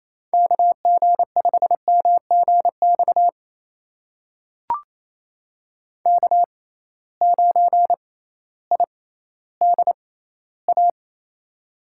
Morse code, heard as a keyed tone, about 27 wpm, spaced as for 15 wpm, 700 Hz. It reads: KG5MGX K9IDA
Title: USA Call Signs - 6x Character Spacing - 28wpm